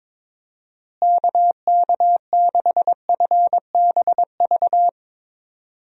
Morse code, heard as a keyed tone, about 22 wpm, 700 Hz. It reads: KK6FBV
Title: USA Call Signs - 6 Characters - Repeated 3x - 22wpm